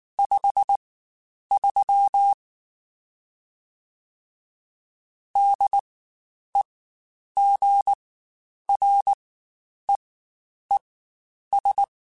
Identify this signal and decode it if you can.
Morse code, heard as a keyed tone, about 19 wpm, spaced as for 9 wpm, 800 Hz. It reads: 53 DEGREES
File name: Spaced Characters / Spaced Characters-mp3s / Weather 5 1 sp @18